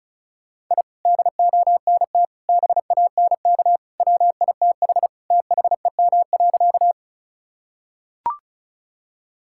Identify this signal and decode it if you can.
Morse code, heard as a keyed tone, about 34 wpm, 700 Hz. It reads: I DONT BANK WITH THEM.
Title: Sentences from Top 1000 Words - Review Part 6 - 35wpm